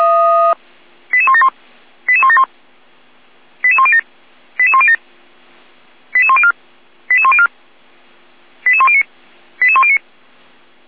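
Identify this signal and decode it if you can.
Five-tone selective call (ZVEI), heard as quick runs of five tones, tones of about 70 ms. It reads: 88171 88171 88107 88107 88104 88104 88109 88109
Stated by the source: ZVEI five-tone selective call 88233 received on SDR.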